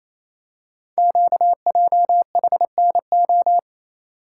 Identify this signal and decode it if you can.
Morse code, heard as a keyed tone, about 28 wpm, 700 Hz. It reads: QJHNO
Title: Code Groups - 5 Letters - 28wpm